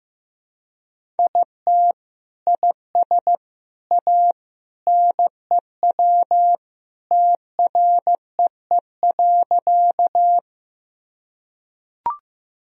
Morse code, heard as a keyed tone, about 16 wpm, 700 Hz. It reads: IT IS A NEW TREE.
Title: Sentences from Top 1000 Words - Review Part 1 - 15wpm